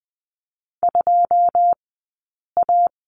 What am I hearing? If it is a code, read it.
Morse code, heard as a keyed tone, about 20 wpm, spaced as for 8 wpm, 700 Hz. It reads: 2A
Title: 080 Lesson 14d - Number 2 - Two Characters 20wpm